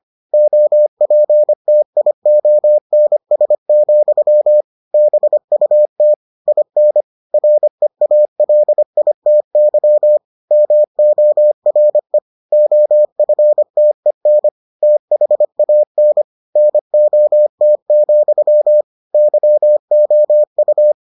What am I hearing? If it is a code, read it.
Morse code, heard as a keyed tone, about 25 wpm, 600 Hz. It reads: OPTIONS, BUT IN REALITY MORE OFTEN THAN NOT, YOU